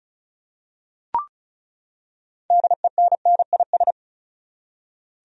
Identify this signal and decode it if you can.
Morse code, heard as a keyed tone, about 36 wpm, 700 Hz. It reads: DENNIS
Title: USA Common Names - 35wpm